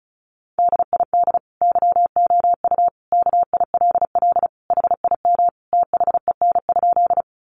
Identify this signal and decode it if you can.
Morse code, heard as a keyed tone, about 34 wpm, 700 Hz. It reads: DID YOU KILL HIM THEN?
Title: Sentences from Top 1400 Words 35wpm